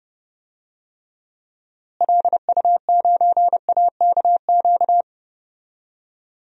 Morse code, heard as a keyed tone, about 30 wpm, 700 Hz. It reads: LU9AKQ
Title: DXCC-100 Call Signs - 6 Characters - Repeated 3x - 30wpm